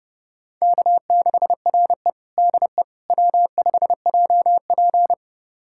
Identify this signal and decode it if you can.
Morse code, heard as a keyed tone, about 30 wpm, 700 Hz. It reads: K6RE DE W5JP
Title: USA Callsign DE Callsign - 4 Characters - 30wpm